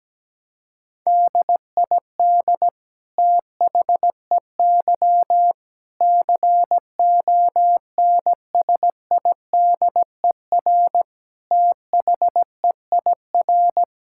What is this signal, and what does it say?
Morse code, heard as a keyed tone, about 17 wpm, 700 Hz. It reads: DID THEY CONSIDER THEIR
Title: Sentences from Top 1000 Words - Review Part 6 - 17wpm